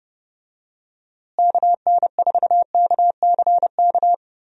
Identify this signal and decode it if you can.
Morse code, heard as a keyed tone, about 32 wpm, 700 Hz. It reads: KN4KCK
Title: USA Call Signs - 6 Characters - Repeated 2x - 30wpm